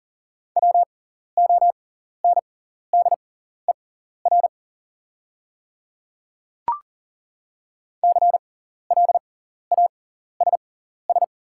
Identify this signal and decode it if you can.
Morse code, heard as a keyed tone, about 40 wpm, spaced as for 14 wpm, 700 Hz. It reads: WONDER CLASS
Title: 6x Character Spacing - Top 500 Words - 40wpm